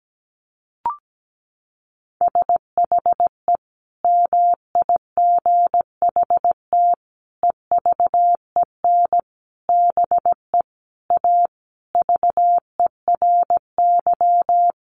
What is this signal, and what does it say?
Morse code, heard as a keyed tone, about 17 wpm, 700 Hz. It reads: SHE MIGHT EVEN BE A VERY
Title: Sentences from Top 1500 Words 17wpm